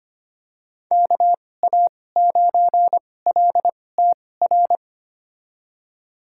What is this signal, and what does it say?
Morse code, heard as a keyed tone, about 25 wpm, spaced as for 18 wpm, 700 Hz. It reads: KA9LTR
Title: USA Call Signs - 6 Characters - 2x Character Spacing - 25wpm